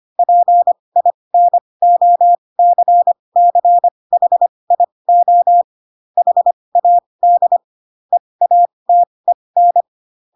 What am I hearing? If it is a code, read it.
Morse code, heard as a keyed tone, about 25 wpm, spaced as for 20 wpm, 700 Hz. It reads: PINOCCHIO HAD EATEN
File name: Pinocchio / Pinocchio ch7 15 25 ik1hge